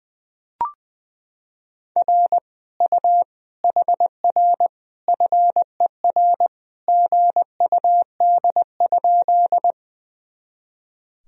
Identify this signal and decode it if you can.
Morse code, heard as a keyed tone, about 21 wpm, 700 Hz. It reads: R U HR FER GUD?
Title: Sentences from Top 200 Words and Abbreviations 20wpm